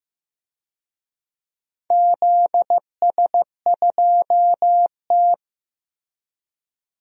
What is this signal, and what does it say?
Morse code, heard as a keyed tone, about 15 wpm, 700 Hz. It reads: ZS2T
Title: DXCC-100 Call Signs - 4 Characters - Repeated 3x - 15wpm